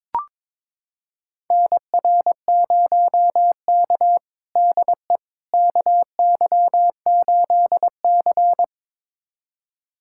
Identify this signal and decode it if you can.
Morse code, heard as a keyed tone, about 22 wpm, 700 Hz. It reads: NR0K DE KY8C
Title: USA Callsign DE Callsign - 4 Characters - 22wpm